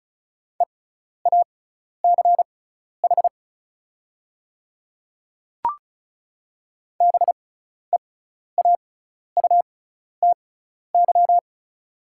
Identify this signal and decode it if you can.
Morse code, heard as a keyed tone, about 36 wpm, spaced as for 12 wpm, 690 Hz. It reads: EACH BEAUTY
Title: 6x Character Spacing - Top 500 Words - 35wpm